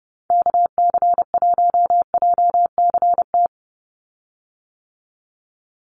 Morse code, heard as a keyed tone, about 30 wpm, 700 Hz. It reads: KC1JCT
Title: USA Call Signs 30wpm